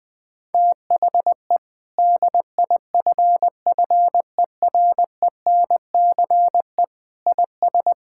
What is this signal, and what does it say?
Morse code, heard as a keyed tone, about 20 wpm, 700 Hz. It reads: THE DIFFERENCE IS
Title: Sentences from Top 1000 Words - Review Part 5 - 20wpm